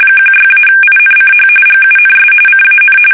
RTTY, about 45.45 baud. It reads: 16-INCH GUNS CAN
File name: fsk850 / battleshipnj